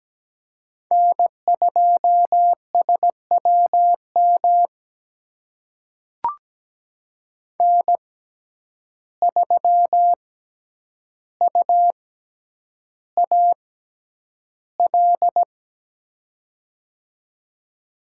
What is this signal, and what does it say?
Morse code, heard as a keyed tone, about 17 wpm, spaced as for 9 wpm, 700 Hz. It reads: N2SWM N3UAL
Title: USA Call Signs - 5 Characters - 6x Character Spacing - 17wpm